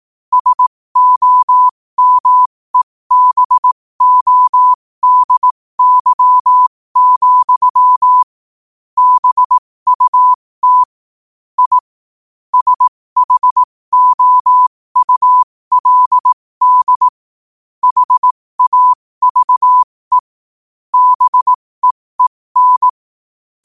Morse code, heard as a keyed tone, about 18 wpm, spaced as for 15 wpm, 1000 Hz. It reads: SOMEBODY, BUT I SHOULD HAVE BEEN
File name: qotdfiles / 15wpm